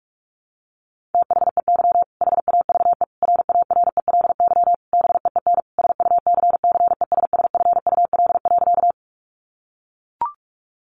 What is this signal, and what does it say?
Morse code, heard as a keyed tone, about 44 wpm, 700 Hz. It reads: THEY HAVE RARELY BEEN SUCCESSFUL.
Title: Sentences from Top 1400 Words 45wpm